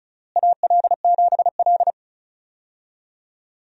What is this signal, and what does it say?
Morse code, heard as a keyed tone, about 36 wpm, 700 Hz. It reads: AL7L
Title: USA Call Signs - 4 Characters - Repeated 3x - 35wpm